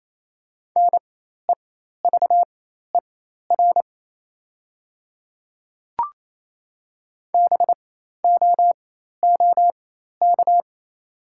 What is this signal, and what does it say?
Morse code, heard as a keyed tone, about 28 wpm, spaced as for 13 wpm, 700 Hz. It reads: NEVER BOOK
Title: 4x Character Spacing - Top 500 Words - 28wpm